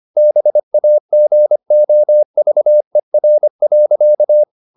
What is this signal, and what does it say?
Morse code, heard as a keyed tone, about 25 wpm, 600 Hz. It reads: BAGOVER.
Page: Ferie 25 wpm